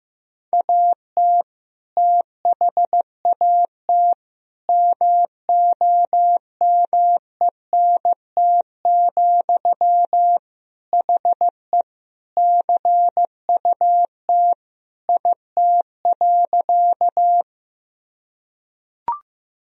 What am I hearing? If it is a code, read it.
Morse code, heard as a keyed tone, about 16 wpm, 700 Hz. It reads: AT THAT MOMENT, HE CUT IT.
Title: Sentences from Top 1000 Words - Review Part 1 - 15wpm